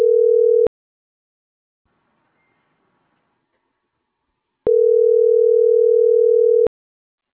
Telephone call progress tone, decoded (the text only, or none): ringback tone